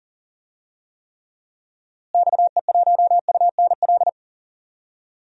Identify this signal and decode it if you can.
Morse code, heard as a keyed tone, about 40 wpm, 700 Hz. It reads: XE1UNL
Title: DXCC-100 Call Signs - 6 Characters - Repeated 3x - 40wpm